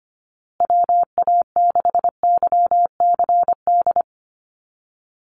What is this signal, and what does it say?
Morse code, heard as a keyed tone, about 25 wpm, 700 Hz. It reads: WA6YCD
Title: USA Call Signs Repeated 2x 25wpm